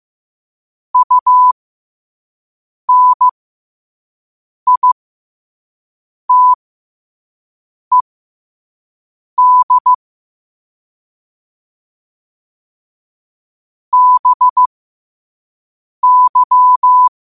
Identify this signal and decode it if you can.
Morse code, heard as a keyed tone, about 15 wpm, spaced as for 5 wpm, 1000 Hz. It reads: UNITED BY
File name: qotdfiles / 5wpm